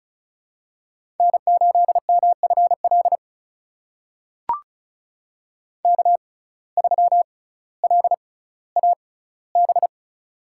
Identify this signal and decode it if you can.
Morse code, heard as a keyed tone, about 36 wpm, spaced as for 18 wpm, 700 Hz. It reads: N8MFL K3LAB
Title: USA Call Signs - 5 Characters - 6x Character Spacing - 35wpm